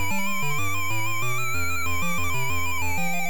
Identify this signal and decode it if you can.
There is no code.